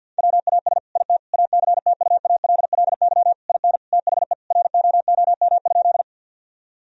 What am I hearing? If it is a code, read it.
Morse code, heard as a keyed tone, about 50 wpm, 700 Hz. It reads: WAS IT ACTUALLY IN THE ROOM?
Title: Sentences from Top 1000 Words - Review Part 4 - 50wpm